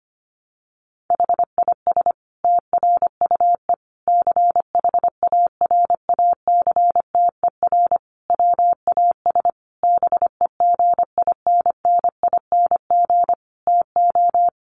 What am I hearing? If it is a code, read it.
Morse code, heard as a keyed tone, about 25 wpm, 700 Hz. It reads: HIS TRUE CHARACTER WAS BEGINNING TO
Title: Sentences from Top 1200 Words 25wpm